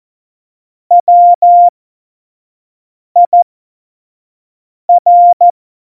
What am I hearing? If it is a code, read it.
Morse code, heard as a keyed tone, about 14 wpm, spaced as for 5 wpm, 700 Hz. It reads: WIR
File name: Morsekurs / Witze / Witz 04 August 2020, 25 F70